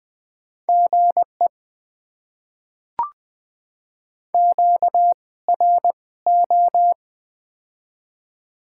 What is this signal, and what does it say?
Morse code, heard as a keyed tone, about 20 wpm, spaced as for 16 wpm, 700 Hz. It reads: GE QRO
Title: Top 90 QSO Elements - 2x Character Spacing - 20wpm